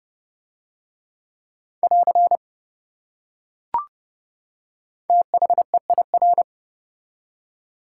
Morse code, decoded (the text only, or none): <AR> THEIR